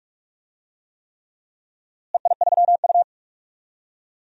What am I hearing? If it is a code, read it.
Morse code, heard as a keyed tone, about 46 wpm, 700 Hz. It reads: EI3U
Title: DXCC-100 Call Signs - 4 Characters - Repeated 3x - 45wpm